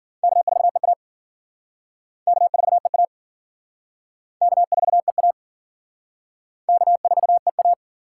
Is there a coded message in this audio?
Yes: Morse code from 0.2 to 7.7 s, heard as a keyed tone, about 52 wpm, 700 Hz. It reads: K4EA K4EA K4EA K4EA